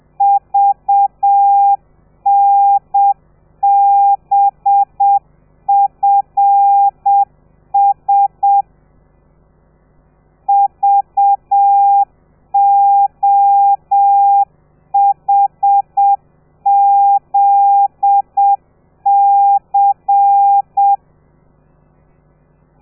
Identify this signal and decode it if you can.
Morse code, heard as a keyed tone, about 7 wpm, 800 Hz. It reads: VNBFS VOHZC
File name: telegrafia / i2tfj / Lez13 30